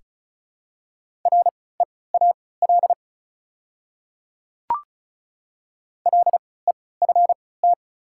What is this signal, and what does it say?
Morse code, heard as a keyed tone, about 36 wpm, spaced as for 20 wpm, 700 Hz. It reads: REAL LEFT
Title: 3x Character Spacing - Top 300 Words - 35wpm